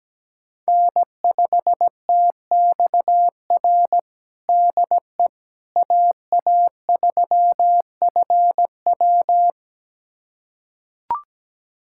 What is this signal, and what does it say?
Morse code, heard as a keyed tone, about 17 wpm, 700 Hz. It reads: N5TXR DE AA3FW
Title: USA Callsign DE Callsign - 5 Characters - 17wpm